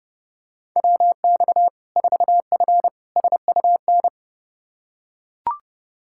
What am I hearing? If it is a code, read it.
Morse code, heard as a keyed tone, about 30 wpm, 700 Hz. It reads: WX 4F SUN
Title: Practice QSO - Exchange 2 - Info - 30wpm